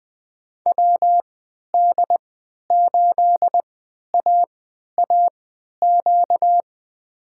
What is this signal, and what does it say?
Morse code, heard as a keyed tone, about 21 wpm, spaced as for 12 wpm, 700 Hz. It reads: WD8AAQ
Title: USA Call Signs - 6 Characters - 3x Character Spacing - 20wpm